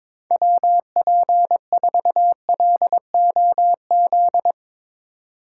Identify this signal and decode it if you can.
Morse code, heard as a keyed tone, about 22 wpm, 700 Hz. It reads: WP4LOZ
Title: USA Call Signs - 3x Character Spacing - 22wpm